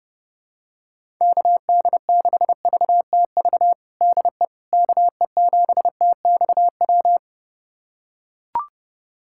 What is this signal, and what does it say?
Morse code, heard as a keyed tone, about 30 wpm, 700 Hz. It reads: KD6VTV DE KE7TXW
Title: USA Callsign DE Callsign 30wpm